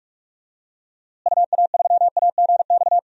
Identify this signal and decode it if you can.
Morse code, heard as a keyed tone, about 44 wpm, 700 Hz. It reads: UA3AGX
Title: DXCC-100 Call Signs - 6 Characters - Repeated 3x - 45wpm